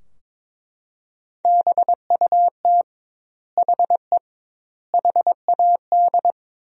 Morse code, heard as a keyed tone, about 22 wpm, 700 Hz. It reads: BUT HE HAD
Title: 2x Word-Spacing - Sentences from Top 100 Words - 22wpm